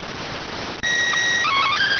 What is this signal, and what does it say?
SSTV (slow-scan television), PD 290.